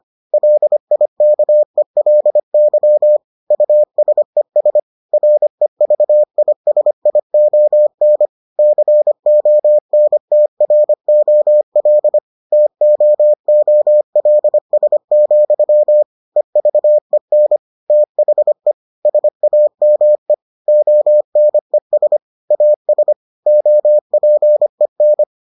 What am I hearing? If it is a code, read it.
Morse code, heard as a keyed tone, about 25 wpm, 600 Hz. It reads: LIKELY USES REVISION CONTROL TOOLS, EVEN THE SAME ONES AS OPEN